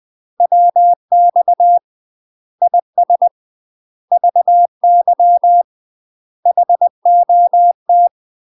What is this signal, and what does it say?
Morse code, heard as a keyed tone, about 20 wpm, 700 Hz. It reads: WX IS VY HOT
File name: UK MASTER QSOs TRAINING / 20WPM / UK MASTER QSOs 0042